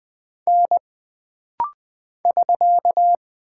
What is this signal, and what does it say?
Morse code, heard as a keyed tone, about 20 wpm, 700 Hz. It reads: N <SK>
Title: Single Character (Letter:Number:Punctuation:Prosign) - Rapid-Fire - 20wpm